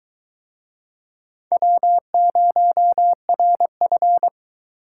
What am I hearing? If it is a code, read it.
Morse code, heard as a keyed tone, about 22 wpm, 700 Hz. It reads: W0RF